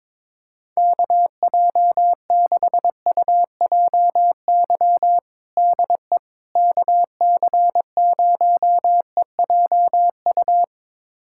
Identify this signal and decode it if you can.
Morse code, heard as a keyed tone, about 22 wpm, 700 Hz. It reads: KJ6UJY DE KC0EJU
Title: USA Callsign DE Callsign - 6 Characters - 22wpm